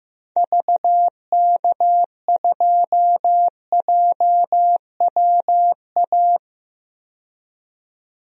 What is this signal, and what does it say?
Morse code, heard as a keyed tone, about 15 wpm, 700 Hz. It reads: VK2JWA
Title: DXCC-100 Call Signs - 6 Characters - 15wpm